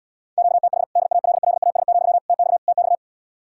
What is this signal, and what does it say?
Morse code, heard as a keyed tone, about 80 wpm, 700 Hz. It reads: QTH DALLAS, TX TX